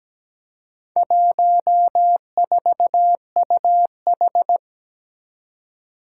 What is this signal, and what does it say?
Morse code, heard as a keyed tone, about 17 wpm, 700 Hz. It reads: 14UH
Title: Four Challenging Letter-Number Combinations 17wpm